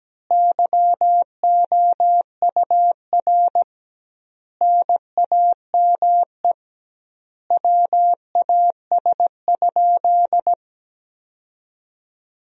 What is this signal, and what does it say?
Morse code, heard as a keyed tone, about 17 wpm, 700 Hz. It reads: YOUR NAME WAS?
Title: 2x Word-Spacing - Sentences from Top 200 Words - 17wpm